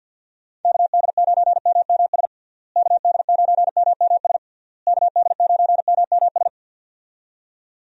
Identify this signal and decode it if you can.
Morse code, heard as a keyed tone, about 50 wpm, 700 Hz. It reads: KD9MMS KD9MMS KD9MMS
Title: USA Call Signs - 6 Characters - Repeated 3x - 50wpm